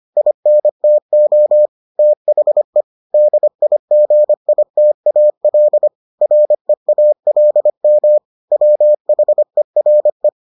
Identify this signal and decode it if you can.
Morse code, heard as a keyed tone, about 25 wpm, 600 Hz. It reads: INTO THE DIGITAL REALM WHERE